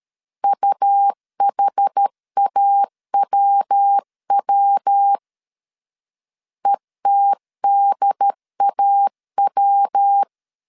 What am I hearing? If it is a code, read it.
Morse code, heard as a keyed tone, about 13 wpm, 800 Hz. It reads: UHAWW ETDAW
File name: CW / DET / 1 3 1500C S A 12wpm 2905